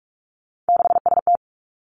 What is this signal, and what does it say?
Morse code, heard as a keyed tone, about 46 wpm, 700 Hz. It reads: 6ST
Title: Three Challenging Letter-Number Combinations 45wpm